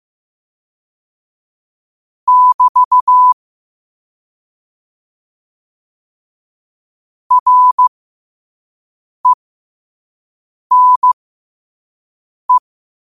Morse code, heard as a keyed tone, about 15 wpm, spaced as for 5 wpm, 1000 Hz. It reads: <BT> RENE
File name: 5wpm